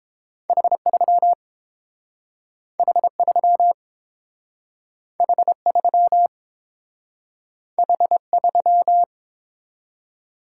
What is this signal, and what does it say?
Morse code, heard as a keyed tone, about 30 wpm, 700 Hz. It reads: H3 H3 H3 H3